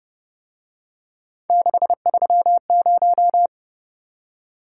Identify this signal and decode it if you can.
Morse code, heard as a keyed tone, about 28 wpm, 700 Hz. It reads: 630